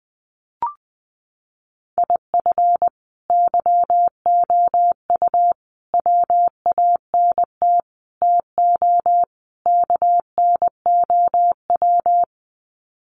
Morse code, heard as a keyed tone, about 20 wpm, 700 Hz. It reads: IF YOU WANT TO KNOW
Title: Sets of 5 Words Encore2 20wpm